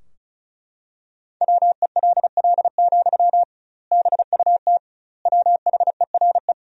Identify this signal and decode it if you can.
Morse code, heard as a keyed tone, about 36 wpm, 700 Hz. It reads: WELL, BUT WHERE
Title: 2x Word-Spacing - Sentences from Top 500 Words - 35wpm